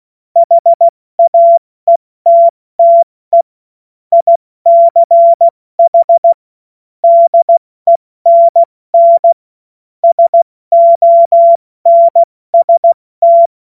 Morse code, heard as a keyed tone, about 16 wpm, 680 Hz. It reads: HAETTE ICH DENN SONST
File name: Morsekurs / Witze / Witz 11. Juli 2023 70 F80